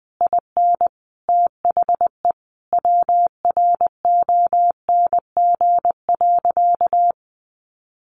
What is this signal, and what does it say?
Morse code, heard as a keyed tone, about 19 wpm, 700 Hz. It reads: IN THE WRONG.
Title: Sentences from Top 1200 Words 20wpm